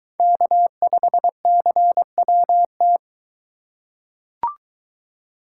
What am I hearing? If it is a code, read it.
Morse code, heard as a keyed tone, about 22 wpm, 700 Hz. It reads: K5CWT